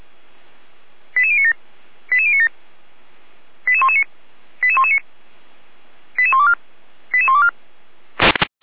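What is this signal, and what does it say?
Five-tone selective call (ZVEI), heard as quick runs of five tones, tones of about 70 ms. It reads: 88097 88097 88119 88119 88124 88124
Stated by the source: ZVEI five-tone call received on SDR.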